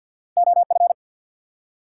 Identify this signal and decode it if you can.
Morse code, heard as a keyed tone, about 53 wpm, 700 Hz. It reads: OF
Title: Top 180 QSO Elements 50wpm